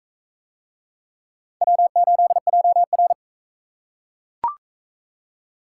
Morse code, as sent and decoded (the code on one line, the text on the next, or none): .-- ---.. .--- .-.
W8JR